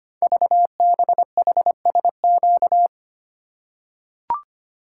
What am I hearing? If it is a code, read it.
Morse code, heard as a keyed tone, about 25 wpm, 700 Hz. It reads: VBHSQ
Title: Code Groups - 5 Letter:Numbers - 25wpm